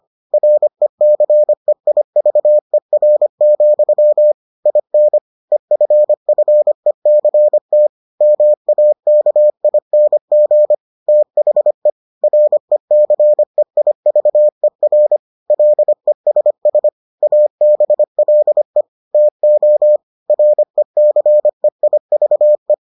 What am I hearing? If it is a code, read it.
Morse code, heard as a keyed tone, about 25 wpm, 600 Hz. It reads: RECEIVER, IN EFFECT MAKING THE RECEIVER LESS ABLE TO RECEIVE